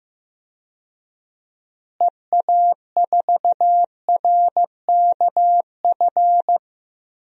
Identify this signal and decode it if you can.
Morse code, heard as a keyed tone, about 15 wpm, 700 Hz. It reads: EA4RKF
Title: DXCC-100 Call Signs - 6 Characters - Repeated 3x - 15wpm